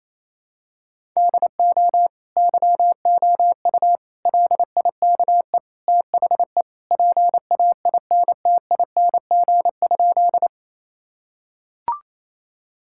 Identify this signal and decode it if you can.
Morse code, heard as a keyed tone, about 28 wpm, 700 Hz. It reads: DO YOU LIKE THE PAINTING?
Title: Sentences from Top 1000 Words - Review Part 6 - 28wpm